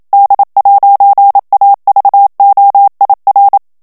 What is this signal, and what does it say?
Morse code, heard as a keyed tone, about 27 wpm, 800 Hz. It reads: D'AVOIR
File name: FRANCAIS TEXTE CLAIR 2 VITESSE 26 MOTS MINUTES